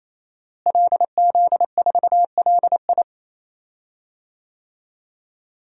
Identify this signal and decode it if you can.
Morse code, heard as a keyed tone, about 28 wpm, 700 Hz. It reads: LZ4LI
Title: DXCC-100 Call Signs - 5 Characters - Repeated 3x - 28wpm